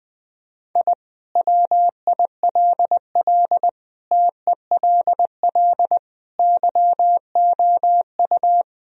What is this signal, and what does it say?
Morse code, heard as a keyed tone, about 21 wpm, 700 Hz. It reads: I WILL TELL YOU
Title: Sentences from Top 1000 Words - Review Part 1 - 20wpm